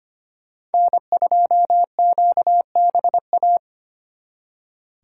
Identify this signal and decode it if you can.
Morse code, heard as a keyed tone, about 30 wpm, 700 Hz. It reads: N2QBA
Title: Speed Racing - USA Call Signs 25wpm